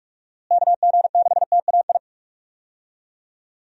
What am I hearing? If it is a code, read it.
Morse code, heard as a keyed tone, about 46 wpm, 700 Hz. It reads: KG6TAI
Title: USA Call Signs - 6 Characters - Repeated 2x - 45wpm